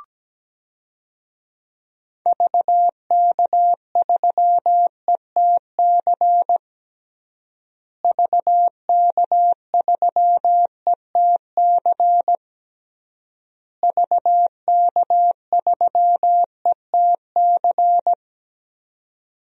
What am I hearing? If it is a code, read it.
Morse code, heard as a keyed tone, about 17 wpm, 700 Hz. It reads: VK3ETC VK3ETC VK3ETC
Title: DXCC-100 Call Signs - 6 Characters - Repeated 3x - 17wpm